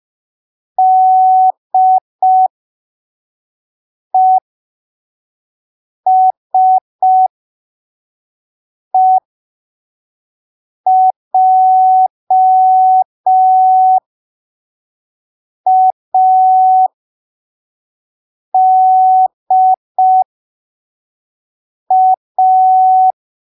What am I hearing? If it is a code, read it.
Morse code, heard as a keyed tone, about 5 wpm, spaced as for 3 wpm, 750 Hz. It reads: DESEJADA
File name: arquivos / texto / texto 04 5ppm